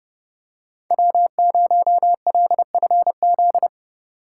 Morse code, heard as a keyed tone, about 31 wpm, 700 Hz. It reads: W0LFZ